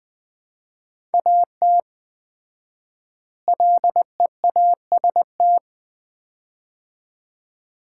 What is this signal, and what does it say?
Morse code, heard as a keyed tone, about 20 wpm, 700 Hz. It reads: AT LEAST
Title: 4x Word-Spacing - Sets of 2 Words - 20wpm